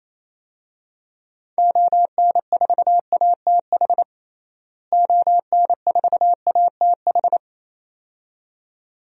Morse code, heard as a keyed tone, about 28 wpm, 700 Hz. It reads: ON4ATH ON4ATH
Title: DXCC-100 Call Signs - 6 Characters - Repeated 2x - 28wpm